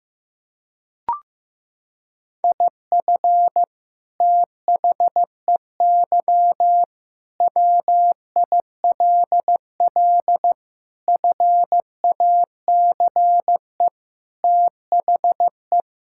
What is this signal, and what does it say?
Morse code, heard as a keyed tone, about 15 wpm, 700 Hz. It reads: IF THEY WILL FACE THE
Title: Sentences from Top 1000 Words - Review Part 6 - 15wpm